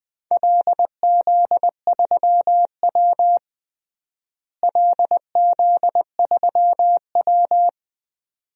Morse code, heard as a keyed tone, about 20 wpm, 700 Hz. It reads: LZ3W LZ3W
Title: DXCC-100 Call Signs - 4 Characters - Repeated 2x - 20wpm